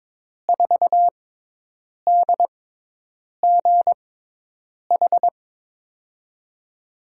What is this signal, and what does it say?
Morse code, heard as a keyed tone, about 22 wpm, spaced as for 8 wpm, 700 Hz. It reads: 4DGH
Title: USA Call Signs - 6 Characters - 6x Character Spacing - 22wpm